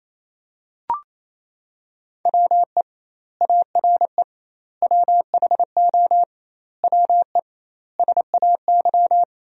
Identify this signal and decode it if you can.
Morse code, heard as a keyed tone, about 28 wpm, 700 Hz. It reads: WE ARE WHO WE SAY
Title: 2x Word-Spacing - Sentences from Top 200 Words - 28wpm